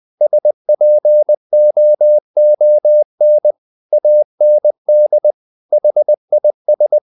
Morse code, heard as a keyed tone, about 20 wpm, 600 Hz. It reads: SPOON AND HIS
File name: the old man and the sea part 2 20wpm